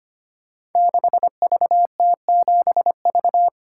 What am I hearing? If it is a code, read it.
Morse code, heard as a keyed tone, about 25 wpm, 700 Hz. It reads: 6VT7V
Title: Code Groups - 5 Letter:Numbers - 25wpm